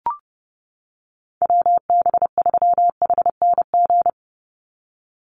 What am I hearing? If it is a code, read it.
Morse code, heard as a keyed tone, about 30 wpm, 700 Hz. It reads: WB3HNG
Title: USA Call Signs 30wpm